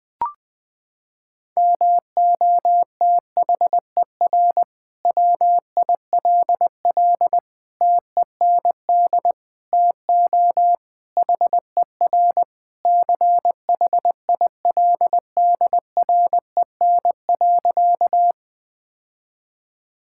Morse code, heard as a keyed tone, about 21 wpm, 700 Hz. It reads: MOTHER WILL TEND TO HER CHILDREN.
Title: Sentences from Top 1000 Words - Review Part 5 - 20wpm